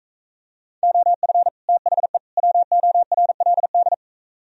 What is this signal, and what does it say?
Morse code, heard as a keyed tone, about 42 wpm, 700 Hz. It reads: OF THE WORLD